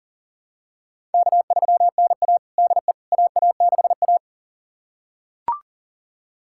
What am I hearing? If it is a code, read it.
Morse code, heard as a keyed tone, about 37 wpm, 700 Hz. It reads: K3NA DE AA6A
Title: USA Callsign DE Callsign - 4 Characters - 40wpm